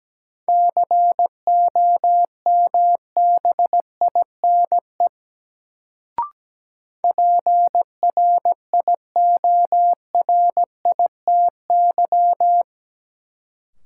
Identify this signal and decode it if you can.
Morse code, heard as a keyed tone, about 17 wpm, 700 Hz. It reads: COMBINE PRIORITY
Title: Sentences from Top 1600 Words - New Word Drill Alt - 17wpm